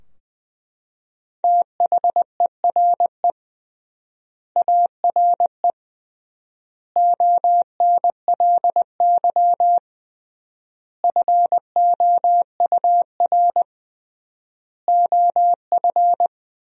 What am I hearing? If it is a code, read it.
Morse code, heard as a keyed tone, about 21 wpm, 700 Hz. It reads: THERE ARE ONLY FOUR OF
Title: 3x Word-Spacing - Sentences from Top 300 Words - 20wpm